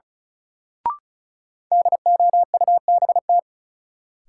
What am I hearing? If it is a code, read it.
Morse code, heard as a keyed tone, about 36 wpm, 700 Hz. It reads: DOUBT
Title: Sentences from Top 1500 Words - New Word Drill Alt - 35wpm